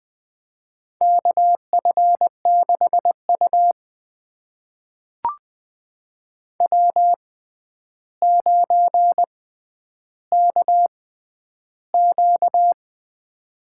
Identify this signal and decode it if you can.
Morse code, heard as a keyed tone, about 21 wpm, spaced as for 10 wpm, 700 Hz. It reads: KF6U W9KQ
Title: USA Call Signs - 4 Characters - 6x Character Spacing - 20wpm